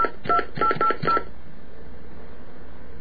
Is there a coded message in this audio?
Yes: DTMF from 0.3 to 1.2 s, heard as telephone touch tones, tones of about 90 ms. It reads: 2880